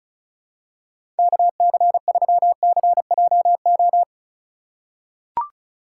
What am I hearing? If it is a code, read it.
Morse code, heard as a keyed tone, about 36 wpm, 700 Hz. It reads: KC3CJO
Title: USA Call Signs - 6 Characters - 2x Character Spacing - 35wpm